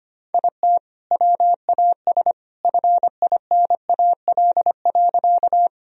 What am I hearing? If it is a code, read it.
Morse code, heard as a keyed tone, about 25 wpm, 700 Hz. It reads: IT WAS FINAL.
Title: Sentences from Top 1000 Words - Review Part 2 - 25wpm